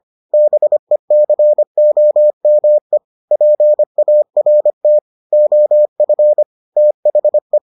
Morse code, heard as a keyed tone, about 25 wpm, 600 Hz. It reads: BECOME PART OF THE